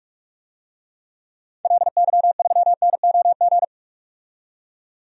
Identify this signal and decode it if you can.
Morse code, heard as a keyed tone, about 46 wpm, 700 Hz. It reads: LY3NOG
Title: DXCC-100 Call Signs - 6 Characters - 45wpm